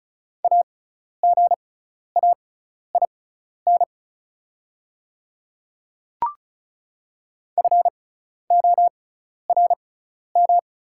Morse code, heard as a keyed tone, about 36 wpm, spaced as for 12 wpm, 690 Hz. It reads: AGAIN FORM
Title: 6x Character Spacing - Top 200 Words - 35wpm